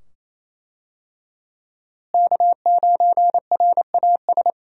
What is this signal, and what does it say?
Morse code, heard as a keyed tone, about 28 wpm, 700 Hz. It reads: K9RAS
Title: USA Call Signs - 5 Characters - Repeated 2x - 28wpm